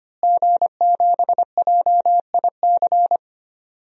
Morse code, heard as a keyed tone, about 25 wpm, 700 Hz. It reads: G7JIC
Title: DXCC-100 Call Signs - 5 Characters - Repeated 2x - 25wpm